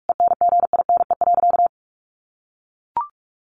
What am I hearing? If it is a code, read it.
Morse code, heard as a keyed tone, about 44 wpm, 700 Hz. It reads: ENGINE.
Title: Sentences from Top 1500 Words 45wpm